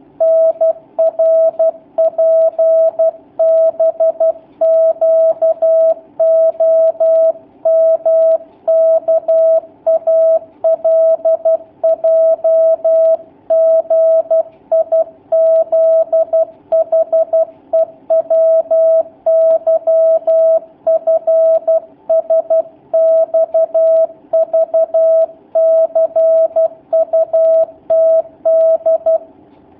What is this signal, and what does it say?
Morse code, heard as a keyed tone, about 12 wpm, 650 Hz. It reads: NRPBQOMKALJGIZHEWYFSXVCUTD
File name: jh8cbh / 2019cw / sp11- w